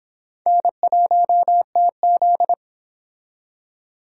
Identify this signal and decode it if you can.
Morse code, heard as a keyed tone, about 26 wpm, 700 Hz. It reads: N1TZ